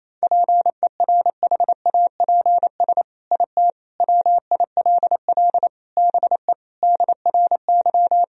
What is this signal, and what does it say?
Morse code, heard as a keyed tone, about 28 wpm, 700 Hz. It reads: PERHAPS IT WILL BE DRY
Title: Sentences from Top 1000 Words - Review Part 3 - 28wpm